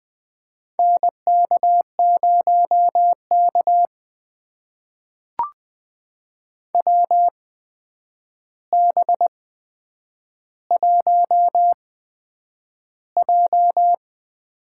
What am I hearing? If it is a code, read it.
Morse code, heard as a keyed tone, about 20 wpm, spaced as for 9 wpm, 700 Hz. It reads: NK0K WB1J
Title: USA Call Signs - 4 Characters - 8x Character Spacing - 20wpm